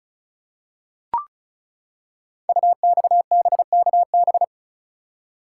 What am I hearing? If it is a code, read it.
Morse code, heard as a keyed tone, about 36 wpm, 700 Hz. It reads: UXBKB
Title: Code Groups - 5 Letters - 35wpm